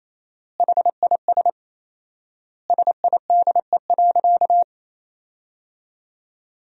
Morse code, heard as a keyed tone, about 28 wpm, 700 Hz. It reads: HIS SIDE.
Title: 4x Word-Spacing - Sentences from Top 400 Words - 28wpm